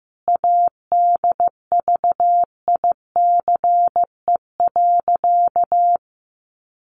Morse code, heard as a keyed tone, about 15 wpm, 700 Hz. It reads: ADVICE.
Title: Sentences from Top 1100 Words 15wpm